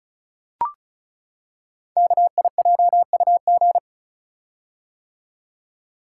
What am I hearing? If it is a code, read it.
Morse code, heard as a keyed tone, about 36 wpm, 690 Hz. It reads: KIJUG
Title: Code Groups - 5 Letters - 35wpm